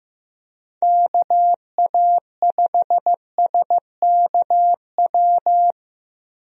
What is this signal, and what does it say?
Morse code, heard as a keyed tone, about 15 wpm, 700 Hz. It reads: KA5SKW
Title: USA Call Signs - 6 Characters - 15wpm